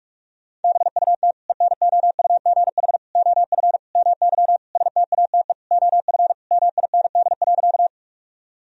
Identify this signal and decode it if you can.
Morse code, heard as a keyed tone, about 41 wpm, 700 Hz. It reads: BUT ENOUGH OF MY STATE OF MIND.
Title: Sentences from Top 1000 Words - Review Part 2 - 45wpm